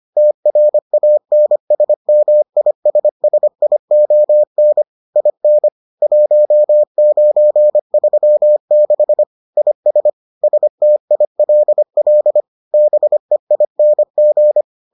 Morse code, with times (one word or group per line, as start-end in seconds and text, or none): 0.2-4.8 TRANSMISSION
5.2-5.7 IN
6.0-9.2 1936
9.6-10.1 IS
10.4-12.4 STILL
12.7-14.6 BEING